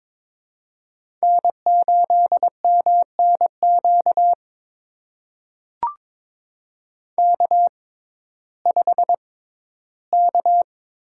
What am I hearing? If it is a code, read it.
Morse code, heard as a keyed tone, about 22 wpm, spaced as for 14 wpm, 700 Hz. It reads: N8MNQ K5K
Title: USA Call Signs - 5 Characters - 6x Character Spacing - 22wpm